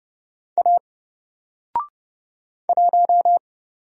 Morse code, heard as a keyed tone, about 30 wpm, 700 Hz. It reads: A 1
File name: Single-Letter-Number-30wpm